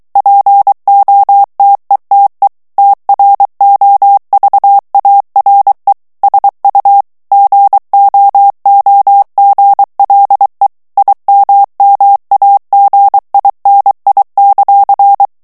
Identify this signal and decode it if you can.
Morse code, heard as a keyed tone, about 23 wpm, 800 Hz. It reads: POTETE TROVARE SU GOOGLE IMMAGINI;
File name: morse mp3 / ITALIEN texte clair 1 vitesse 22 mots minute